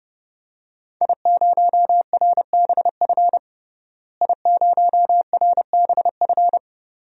Morse code, heard as a keyed tone, about 30 wpm, 700 Hz. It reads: I0RBF I0RBF
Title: DXCC-100 Call Signs - 5 Characters - Repeated 2x - 30wpm